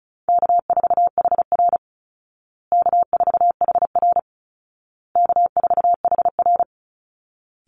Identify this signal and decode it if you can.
Morse code, heard as a keyed tone, about 34 wpm, 700 Hz. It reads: K4HR K4HR K4HR
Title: Easy Call Signs Repeated 3x - 35wpm